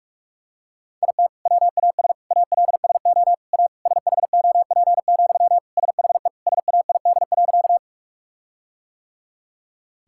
Morse code, heard as a keyed tone, about 46 wpm, 700 Hz. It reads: IT WAS ALSO A SHOP, SHE SAID.
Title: Sentences from Top 1600 Words 45wpm